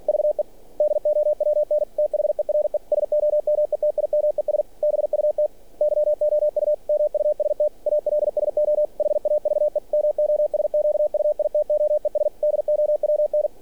Morse code, heard as a keyed tone, about 46 wpm, 600 Hz. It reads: VE DOWN THERE SOMETIMES BUT YOU MUST ALSO HAVE MOSQUITOES DOWN
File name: archived / CRSnet